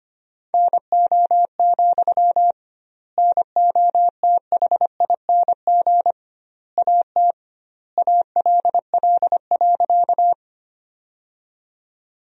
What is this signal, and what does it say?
Morse code, heard as a keyed tone, about 25 wpm, 700 Hz. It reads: NO, NOTHING AT ALL.
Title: 2x Word-Spacing - Sentences from Top 500 Words - 25wpm